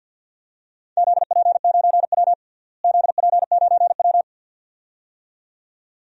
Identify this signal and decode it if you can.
Morse code, heard as a keyed tone, about 49 wpm, 700 Hz. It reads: ZP9W ZP9W
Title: DXCC-100 Call Signs - 4 Characters - Repeated 2x - 50wpm